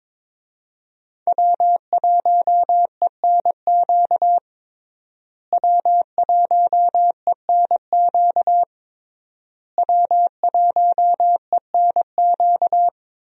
Morse code, heard as a keyed tone, about 22 wpm, 700 Hz. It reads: W1ENQ W1ENQ W1ENQ
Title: USA Call Signs - 5 Characters - Repeated 3x - 22wpm